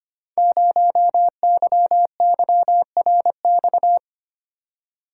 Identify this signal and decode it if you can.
Morse code, heard as a keyed tone, about 25 wpm, 700 Hz. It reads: 0YYRX
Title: Code Groups - 5 Letter:Numbers - 25wpm